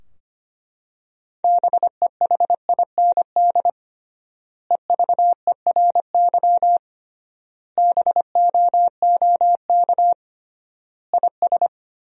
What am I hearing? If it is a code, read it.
Morse code, heard as a keyed tone, about 25 wpm, 700 Hz. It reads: BEHIND EVERY BOOK IS
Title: 3x Word-Spacing - Sentences from Top 500 Words - 25wpm